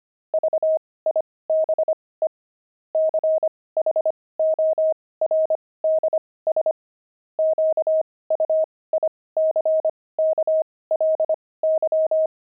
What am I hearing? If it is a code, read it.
Morse code, heard as a keyed tone, about 25 wpm, spaced as for 18 wpm, 630 Hz. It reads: VIBE CHORDS QUICKLY
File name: pangram-fs-18-cs-25-jazz